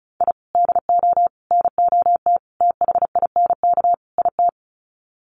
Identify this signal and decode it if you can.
Morse code, heard as a keyed tone, about 34 wpm, 700 Hz. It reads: I DO NOT THINK IT
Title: Sets of 5 Words 35wpm